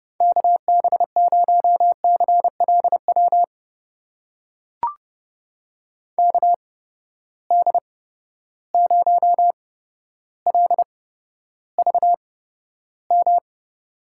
Morse code, heard as a keyed tone, about 31 wpm, spaced as for 13 wpm, 700 Hz. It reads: KB0CLW KD0LVM
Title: USA Call Signs - 6 Characters - 8x Character Spacing - 30wpm